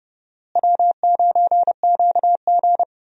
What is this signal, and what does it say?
Morse code, heard as a keyed tone, about 30 wpm, 700 Hz. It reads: W9QG
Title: USA Call Signs - 4 Characters - Repeated 2x - 30wpm